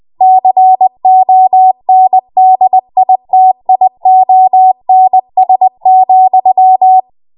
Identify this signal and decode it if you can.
Morse code, heard as a keyed tone, about 20 wpm, 750 Hz. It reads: CONDITIONS,
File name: Morse / 20WPM